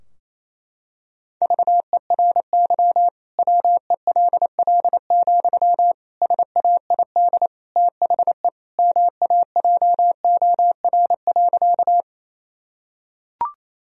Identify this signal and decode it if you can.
Morse code, heard as a keyed tone, about 28 wpm, 700 Hz. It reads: VERY WELL, SAID THE MAJOR.
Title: Sentences from Top 1000 Words - Review Part 4 - 28wpm